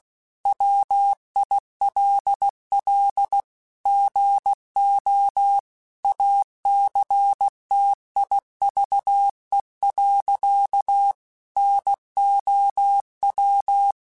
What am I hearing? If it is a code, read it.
Morse code, heard as a keyed tone, about 16 wpm, 800 Hz. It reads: WILL GO ACTIVE. NOW
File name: Technical Subjects / Technical Subjects-mp3s / AND Gate @15